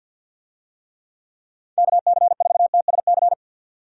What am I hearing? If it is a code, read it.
Morse code, heard as a keyed tone, about 50 wpm, 700 Hz. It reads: KC4TSC
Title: USA Call Signs - 6 Characters - 50wpm